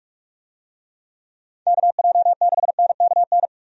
Morse code, heard as a keyed tone, about 46 wpm, 700 Hz. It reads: KJ6NKN
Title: USA Call Signs - 6 Characters - 45wpm